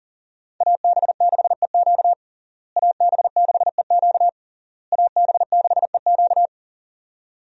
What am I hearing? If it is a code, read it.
Morse code, heard as a keyed tone, about 37 wpm, 700 Hz. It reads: AB6EQ AB6EQ AB6EQ
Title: USA Call Signs - 5 Characters - Repeated 3x - 40wpm